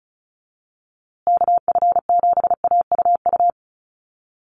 Morse code, heard as a keyed tone, about 34 wpm, 700 Hz. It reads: KF7AUU
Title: USA Call Signs Repeated 3x 35wpm